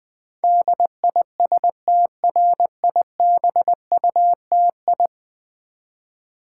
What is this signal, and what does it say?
Morse code, heard as a keyed tone, about 20 wpm, 700 Hz. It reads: DISTRIBUTI
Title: Build A Word - 5 Letters At A Time - 12-Letter Words - 20wpm